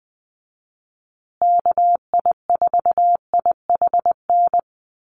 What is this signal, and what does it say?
Morse code, heard as a keyed tone, about 20 wpm, 700 Hz. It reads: KI4IHN
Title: USA Call Signs Repeated 3x 20wpm